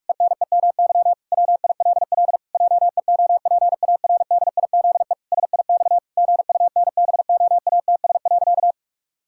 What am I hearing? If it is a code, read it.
Morse code, heard as a keyed tone, about 46 wpm, 700 Hz. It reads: ENEMY WILL JEOPARDIZE SIX GUNBOATS.
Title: Pangram Challenge v2 - 45wpm